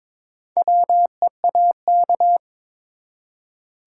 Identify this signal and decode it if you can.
Morse code, heard as a keyed tone, about 22 wpm, 700 Hz. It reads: WEAK
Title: Four Letter Words - 22wpm